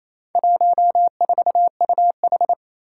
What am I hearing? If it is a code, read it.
Morse code, heard as a keyed tone, about 28 wpm, 700 Hz. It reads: 14UH
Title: Four Challenging Letter-Number Combinations 28wpm